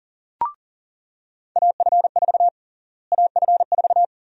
Morse code, heard as a keyed tone, about 40 wpm, 700 Hz. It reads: AF4 AF4
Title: USA Call Sign Prefixes - Repeated 2x - 40wpm